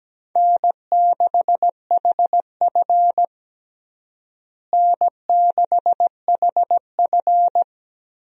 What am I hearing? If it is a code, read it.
Morse code, heard as a keyed tone, about 17 wpm, 700 Hz. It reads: N6HF N6HF
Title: USA Call Signs - 4 Characters - Repeated 2x - 17wpm